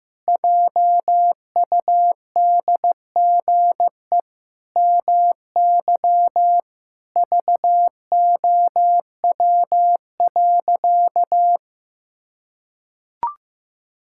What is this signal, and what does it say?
Morse code, heard as a keyed tone, about 16 wpm, 700 Hz. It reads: JUDGE MY VOW.
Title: Pangram Challenge v2 - 15wpm